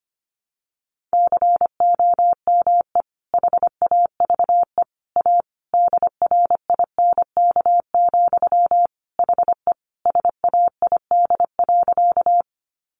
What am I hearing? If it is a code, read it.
Morse code, heard as a keyed tone, about 25 wpm, 700 Hz. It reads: COME HAVE A DRINK, HE SAID.
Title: Sentences from Top 1500 Words 25wpm